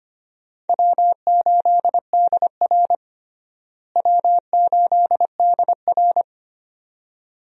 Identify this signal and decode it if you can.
Morse code, heard as a keyed tone, about 25 wpm, 700 Hz. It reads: W8DR W8DR
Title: USA Call Signs - 4 Characters - Repeated 2x - 25wpm